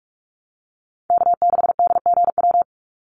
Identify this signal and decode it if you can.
Morse code, heard as a keyed tone, about 44 wpm, 700 Hz. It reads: K6DGW
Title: Call Signs Repeated 3x - 45wpm